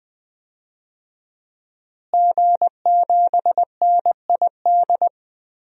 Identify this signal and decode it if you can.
Morse code, heard as a keyed tone, about 20 wpm, 700 Hz. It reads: G7NID
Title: DXCC-100 Call Signs - 5 Characters - Repeated 3x - 20wpm